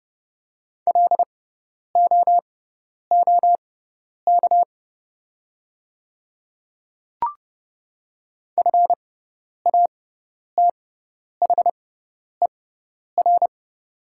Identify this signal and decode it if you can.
Morse code, heard as a keyed tone, about 30 wpm, spaced as for 10 wpm, 690 Hz. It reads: LOOK FATHER
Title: 6x Character Spacing - Top 200 Words - 30wpm